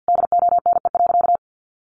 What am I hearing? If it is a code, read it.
Morse code, heard as a keyed tone, about 50 wpm, 700 Hz. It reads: DONE.
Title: Sentences from Top 1500 Words 50wpm